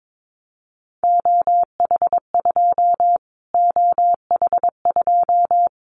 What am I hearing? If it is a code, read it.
Morse code, heard as a keyed tone, about 22 wpm, 700 Hz. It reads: OH2 OH2
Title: Callsign Prefixes for Top 25 DXCC - Repeated 3x - 22wpm